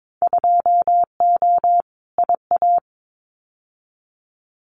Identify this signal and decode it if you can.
Morse code, heard as a keyed tone, about 22 wpm, 700 Hz. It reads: 2O IA
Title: Winter Field Day Exchanges 22wpm